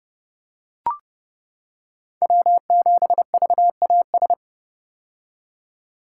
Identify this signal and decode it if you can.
Morse code, heard as a keyed tone, about 30 wpm, 700 Hz. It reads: W7VAS
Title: K1USN SST QSO Practice - 30wpm